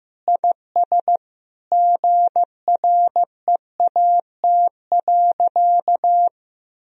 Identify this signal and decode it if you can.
Morse code, heard as a keyed tone, about 15 wpm, 700 Hz. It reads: IS GREAT.
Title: Sentences from Top 1000 Words - Review Part 4 - 15wpm